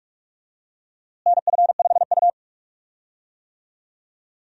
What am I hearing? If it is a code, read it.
Morse code, heard as a keyed tone, about 47 wpm, 700 Hz. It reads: NF5U
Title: USA Call Signs - 4 Characters - Repeated 3x - 45wpm